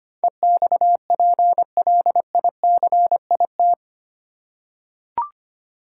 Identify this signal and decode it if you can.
Morse code, heard as a keyed tone, about 25 wpm, 700 Hz. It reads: EXPLICIT
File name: 8 Letter Words - First 3 Letters Given - 25wpm